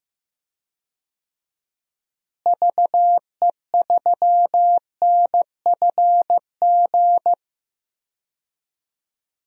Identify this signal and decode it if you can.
Morse code, heard as a keyed tone, about 15 wpm, 700 Hz. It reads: VE3NFG
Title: DXCC-100 Call Signs - 6 Characters - Repeated 3x - 15wpm